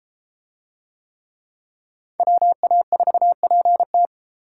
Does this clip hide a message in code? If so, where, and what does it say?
Yes: Morse code from 2.2 to 4.1 s, heard as a keyed tone, about 33 wpm, 700 Hz. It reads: WA4PT